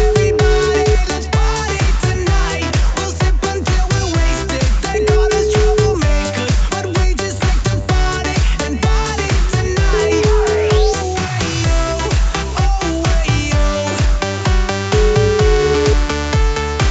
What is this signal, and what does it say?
Ringback tone (Europe, 425 Hz), with noise.